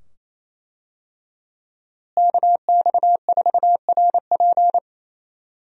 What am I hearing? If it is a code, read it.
Morse code, heard as a keyed tone, about 28 wpm, 700 Hz. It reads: KX4RP
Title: USA Call Signs - 5 Characters - Repeated 2x - 28wpm